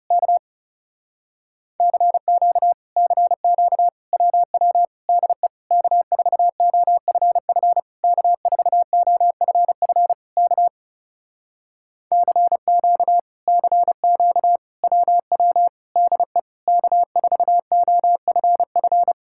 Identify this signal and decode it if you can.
Morse code, heard as a keyed tone, about 37 wpm, 700 Hz. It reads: K CQ CQ WW DE K4OFF K4OFF K CQ CQ WW DE K4OFF
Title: Speed-Racing Common QSO Phrases Encore - 30wpm